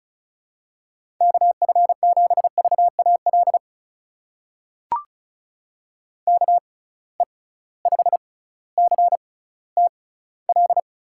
Morse code, heard as a keyed tone, about 36 wpm, spaced as for 18 wpm, 700 Hz. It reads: KF7VAL KE5CTL
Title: USA Call Signs - 6 Characters - 6x Character Spacing - 35wpm